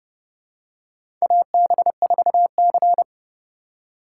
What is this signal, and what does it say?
Morse code, heard as a keyed tone, about 30 wpm, 700 Hz. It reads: AB4C
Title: USA Call Signs - 4 Characters - 6x Character Spacing - 30wpm